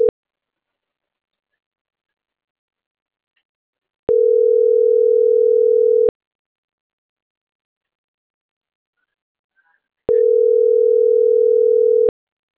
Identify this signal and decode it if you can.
Ringback tone (North America), 440 + 480 Hz.
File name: CENTRAL NAVIERAS / out roberto.rueda